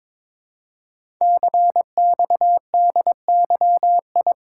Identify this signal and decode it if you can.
Morse code, heard as a keyed tone, about 22 wpm, 700 Hz. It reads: CXDYI
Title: Code Groups - 5 Letters - 22wpm